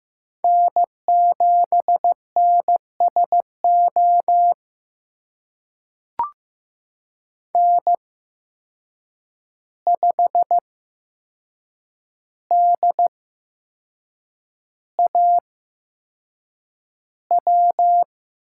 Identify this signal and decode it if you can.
Morse code, heard as a keyed tone, about 15 wpm, spaced as for 6 wpm, 700 Hz. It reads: N7NSO N5DAW
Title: USA Call Signs - 5 Characters - 8x Character Spacing - 15wpm